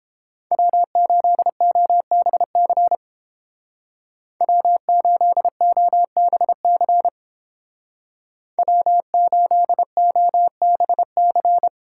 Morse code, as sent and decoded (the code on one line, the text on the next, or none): .-- ---.. --- -... -.-. / .-- ---.. --- -... -.-. / .-- ---.. --- -... -.-.
W8OBC W8OBC W8OBC